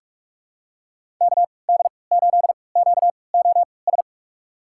Morse code, heard as a keyed tone, about 43 wpm, spaced as for 25 wpm, 700 Hz. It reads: KD8QOS
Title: USA Call Signs - 6 Characters - 3x Character Spacing - 45wpm